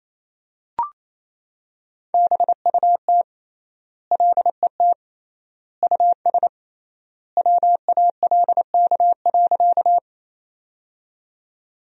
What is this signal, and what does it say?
Morse code, heard as a keyed tone, about 28 wpm, 700 Hz. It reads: BUT LET US WALK.
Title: 3x Word-Spacing - Sentences from Top 300 Words - 28wpm